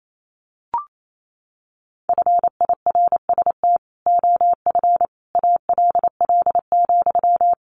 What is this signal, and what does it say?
Morse code, heard as a keyed tone, about 28 wpm, 700 Hz. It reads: FIRST OF ALL,
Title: Sentences from Top 1500 Words 28wpm